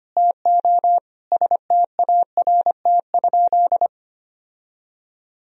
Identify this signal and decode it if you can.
Morse code, heard as a keyed tone, about 25 wpm, 700 Hz. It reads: TO START?
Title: Sentences from Top 1000 Words - Review Part 1 - 25wpm